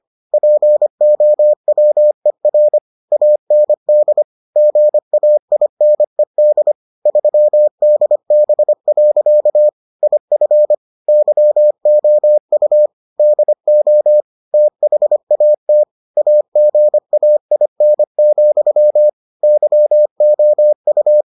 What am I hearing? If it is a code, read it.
Morse code, heard as a keyed tone, about 25 wpm, 600 Hz. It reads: POWER AND GAINED 3DB. IF YOU DO THAT AGAIN, YOU